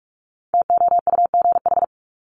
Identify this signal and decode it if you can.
Morse code, heard as a keyed tone, about 44 wpm, 700 Hz. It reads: TOUGH
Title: Top 1001-1250 Words 45wpm